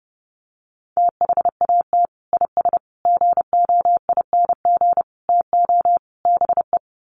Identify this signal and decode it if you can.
Morse code, heard as a keyed tone, about 28 wpm, 700 Hz. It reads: THAT IS GOING TO BE
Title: Sets of 5 Words Encore2 30wpm